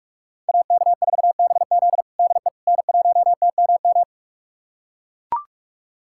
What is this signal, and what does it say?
Morse code, heard as a keyed tone, about 41 wpm, 700 Hz. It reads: AK4BZ DE N1TMM
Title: USA Callsign DE Callsign - 5 Characters - 45wpm